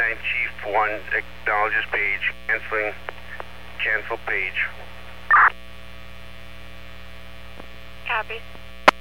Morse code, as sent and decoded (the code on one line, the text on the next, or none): none